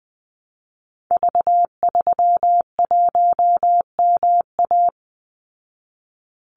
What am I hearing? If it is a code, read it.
Morse code, heard as a keyed tone, about 20 wpm, 700 Hz. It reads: V31MA
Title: Call Signs Repeated 3x - 20wpm